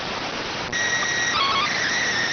SSTV (slow-scan television), PD 290.